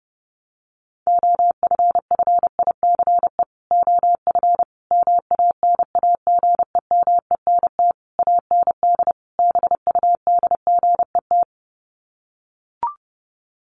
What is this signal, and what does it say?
Morse code, heard as a keyed tone, about 30 wpm, 700 Hz. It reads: OFFICE OF MANAGEMENT AND BUDGET
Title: Sets of 5 Words Encore2 30wpm